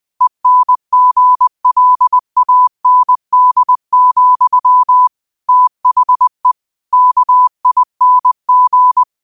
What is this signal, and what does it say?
Morse code, heard as a keyed tone, about 20 wpm, 1000 Hz. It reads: ENGLAND, THE KING
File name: qotdfiles / 20wpm